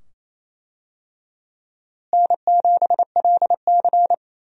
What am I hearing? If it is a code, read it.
Morse code, heard as a keyed tone, about 28 wpm, 700 Hz. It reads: N7LC
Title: USA Call Signs - 4 Characters - Repeated 2x - 28wpm